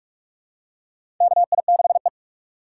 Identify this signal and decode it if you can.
Morse code, heard as a keyed tone, about 46 wpm, 700 Hz. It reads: KI6E
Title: USA Call Signs - 4 Characters - 2x Character Spacing - 45wpm